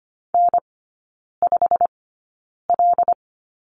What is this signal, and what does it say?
Morse code, heard as a keyed tone, about 25 wpm, spaced as for 9 wpm, 690 Hz. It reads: N5L
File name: Lesson 15e - Number 5 - Three Characters 25wpm